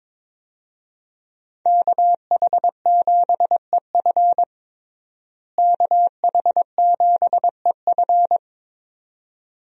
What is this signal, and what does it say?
Morse code, heard as a keyed tone, about 22 wpm, 700 Hz. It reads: KH7EF KH7EF
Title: USA Call Signs - 5 Characters - Repeated 2x - 22wpm